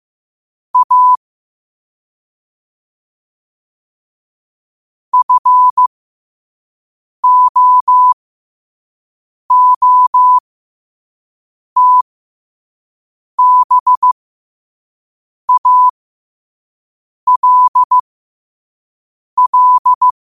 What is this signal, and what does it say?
Morse code, heard as a keyed tone, about 15 wpm, spaced as for 5 wpm, 1000 Hz. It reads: A FOOTBALL
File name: qotdfiles / 5wpm